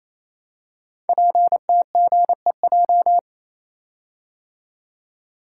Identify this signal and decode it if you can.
Morse code, heard as a keyed tone, about 27 wpm, 700 Hz. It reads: PTGEJ
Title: Code Groups - 5 Letters - 28wpm